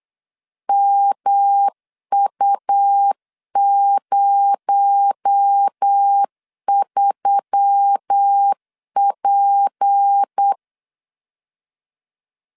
Morse code, heard as a keyed tone, about 8 wpm, 800 Hz. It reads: MU03P
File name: LEZIONI / 6 PKX0 / 6 PKX0 8wpm 3341